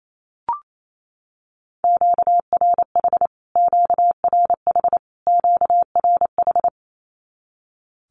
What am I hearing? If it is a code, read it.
Morse code, heard as a keyed tone, about 28 wpm, 700 Hz. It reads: QRH QRH QRH
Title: Q-Codes Repeated 3x 28wpm